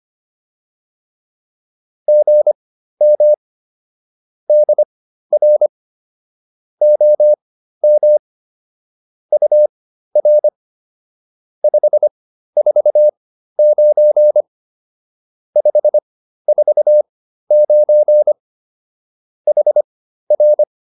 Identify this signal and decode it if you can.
Morse code, heard as a keyed tone, about 25 wpm, spaced as for 13 wpm, 600 Hz. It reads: GM DR OM UR 549 549 HR